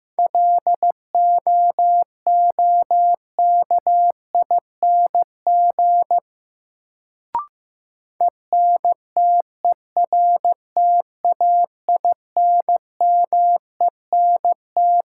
Morse code, heard as a keyed tone, about 15 wpm, 700 Hz. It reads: LOOKING ENTERTAINMENT
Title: Sentences from Top 1000 Words - Review Part 6 - New Word Drill - 15wpm